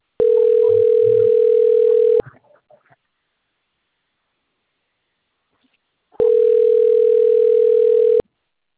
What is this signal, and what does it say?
Ringback tone (North America), 440 + 480 Hz.
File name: out- - - - - .27555